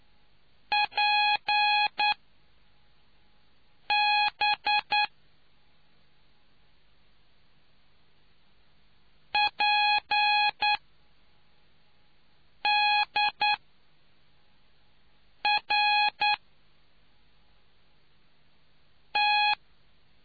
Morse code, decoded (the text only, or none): P B P D R T